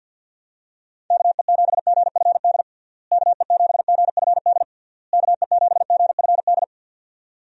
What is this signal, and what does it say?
Morse code, heard as a keyed tone, about 52 wpm, 700 Hz. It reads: KE7GFD KE7GFD KE7GFD
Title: USA Call Signs - 6 Characters - Repeated 3x - 50wpm